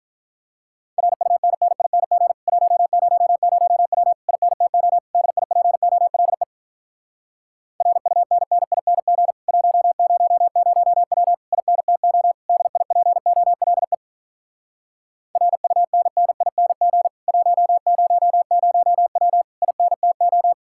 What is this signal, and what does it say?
Morse code, heard as a keyed tone, about 48 wpm, 700 Hz. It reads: RUNNING 100W INTO DIPOLE RUNNING 100W INTO DIPOLE RUNNING 100W INTO